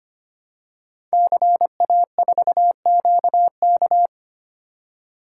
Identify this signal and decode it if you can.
Morse code, heard as a keyed tone, about 25 wpm, 700 Hz. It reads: CA4QK
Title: Code Groups - 5 Letter:Numbers - 25wpm